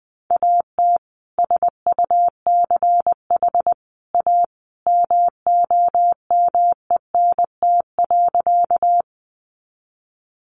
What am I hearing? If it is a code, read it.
Morse code, heard as a keyed tone, about 20 wpm, 700 Hz. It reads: AT SUCH A MOMENT.
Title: Sentences from Top 1000 Words 20wpm